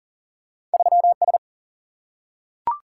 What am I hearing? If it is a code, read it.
Morse code, heard as a keyed tone, about 39 wpm, 700 Hz. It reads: 3S
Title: Single-Letter Single-Number 40wpm